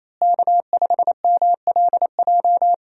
Morse code, heard as a keyed tone, about 28 wpm, 700 Hz. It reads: K5MLJ
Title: USA Call Signs - 5 Characters - Repeated 2x - 28wpm